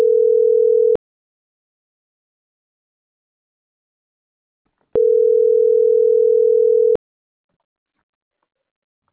Ringback tone (North America), 440 + 480 Hz.